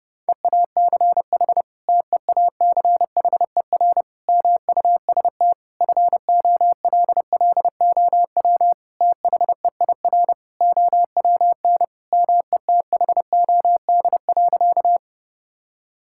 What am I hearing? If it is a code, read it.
Morse code, heard as a keyed tone, about 30 wpm, 700 Hz. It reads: EACH TEACHER MUST FOLLOW THEIR OWN METHOD.
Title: Sentences from Top 1000 Words - Review Part 6 - 30wpm